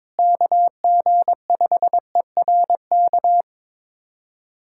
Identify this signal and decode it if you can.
Morse code, heard as a keyed tone, about 21 wpm, 700 Hz. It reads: KG5ERK
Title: USA Callsign DE Callsign - 6 Characters - 22wpm